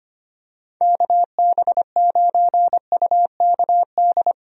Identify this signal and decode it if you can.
Morse code, heard as a keyed tone, about 25 wpm, 700 Hz. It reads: KB9UKD
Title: USA Call Signs - 6 Characters - Repeated 3x - 25wpm